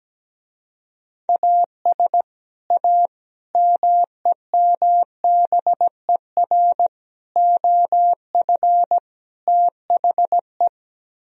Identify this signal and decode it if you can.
Morse code, heard as a keyed tone, about 17 wpm, 700 Hz. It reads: AS A MEMBER OF THE
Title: Speed-Racing - Sets of 5 Words 17wpm